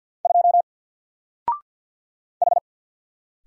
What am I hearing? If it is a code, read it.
Morse code, heard as a keyed tone, about 48 wpm, 700 Hz. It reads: 2 H
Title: Single Character (Letter:Number:Punctuation:Prosign) 50wpm